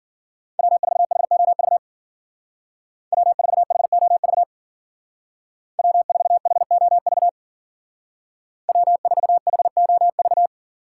Morse code, heard as a keyed tone, about 49 wpm, 700 Hz. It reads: W4HOV W4HOV W4HOV W4HOV
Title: Speed Racing - USA 1x3 Call Signs 40wpm